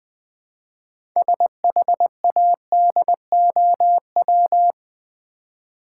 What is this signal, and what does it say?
Morse code, heard as a keyed tone, about 20 wpm, 700 Hz. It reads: SHADOW
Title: Build A Word - 3 Letters At A Time - 6-Letter Words - 20wpm